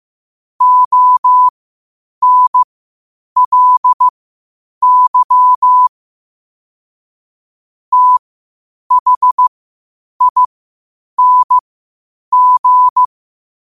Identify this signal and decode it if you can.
Morse code, heard as a keyed tone, about 15 wpm, spaced as for 8 wpm, 1000 Hz. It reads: ONLY THING
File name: qotdfiles / 8wpm